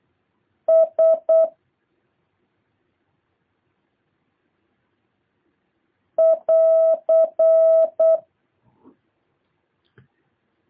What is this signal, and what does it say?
Morse code, heard as a keyed tone, about 8 wpm, 650 Hz. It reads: S <AR>